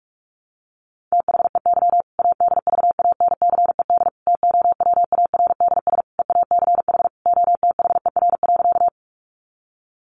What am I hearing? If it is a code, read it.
Morse code, heard as a keyed tone, about 44 wpm, 700 Hz. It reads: THEY ADVANCED TOWARDS EACH OTHER.
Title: Sentences from Top 1200 Words 45wpm